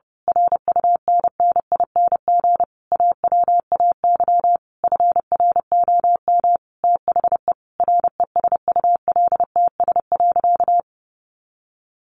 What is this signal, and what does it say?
Morse code, heard as a keyed tone, about 30 wpm, 700 Hz. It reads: RUNNING AWAY FROM THE RESULTS.
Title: A Sound Of Thunder 30wpm